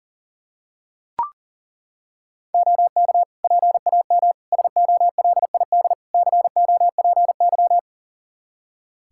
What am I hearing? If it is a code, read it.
Morse code, heard as a keyed tone, about 37 wpm, 700 Hz. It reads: OK PAM SOLID COPY
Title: Practice QSO - Exchange 3 - More Info - 40wpm